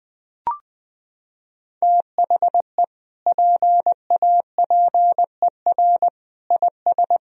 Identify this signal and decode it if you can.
Morse code, heard as a keyed tone, about 20 wpm, 700 Hz. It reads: THE PAPER IS
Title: Sentences from Top 1000 Words - Review Part 1 - 20wpm